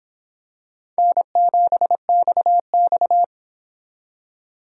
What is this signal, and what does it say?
Morse code, heard as a keyed tone, about 26 wpm, 700 Hz. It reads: N7XX